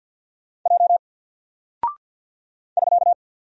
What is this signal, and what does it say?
Morse code, heard as a keyed tone, about 48 wpm, 700 Hz. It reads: J <SK>
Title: Single Character (Letter:Number:Punctuation:Prosign) 50wpm